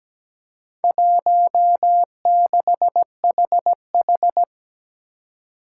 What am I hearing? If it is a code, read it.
Morse code, heard as a keyed tone, about 17 wpm, 700 Hz. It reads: 16HH
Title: Four Challenging Letter-Number Combinations 17wpm